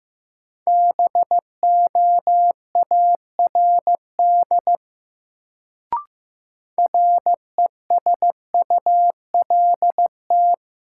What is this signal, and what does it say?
Morse code, heard as a keyed tone, about 15 wpm, 700 Hz. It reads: BOARD RESULT
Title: Sentences from Top 700 Words - New Word Drill - 15wpm